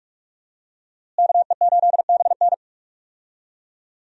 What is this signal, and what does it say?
Morse code, heard as a keyed tone, about 44 wpm, 700 Hz. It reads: KE8BN
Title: USA Call Signs - 5 Characters - Repeated 3x - 45wpm